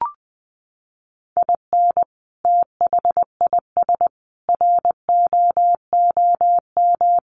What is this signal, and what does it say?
Morse code, heard as a keyed tone, about 20 wpm, 700 Hz. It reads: IN THIS ROOM
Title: Sentences from Top 700 Words 20wpm